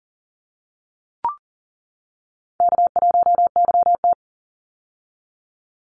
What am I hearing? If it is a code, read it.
Morse code, heard as a keyed tone, about 40 wpm, 700 Hz. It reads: K1YT
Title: Easy Call Signs - 40wpm